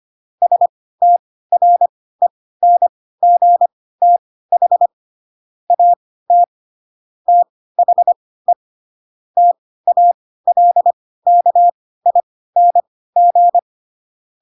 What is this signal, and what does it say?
Morse code, heard as a keyed tone, about 25 wpm, spaced as for 16 wpm, 700 Hz. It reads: STRENGTH AT THE TALKING
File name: Pinocchio / Pinocchio ch4 10 25 ik1hge